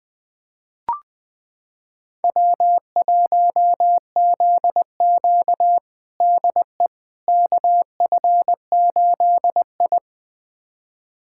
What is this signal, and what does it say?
Morse code, heard as a keyed tone, about 20 wpm, 700 Hz. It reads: W1ZQ DE KF8I
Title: USA Callsign DE Callsign - 4 Characters - 20wpm